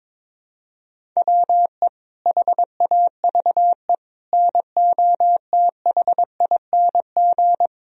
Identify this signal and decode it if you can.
Morse code, heard as a keyed tone, about 22 wpm, 700 Hz. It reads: WE HAVE NOTHING
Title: Sentences from Top 1000 Words - Review Part 1 - 22wpm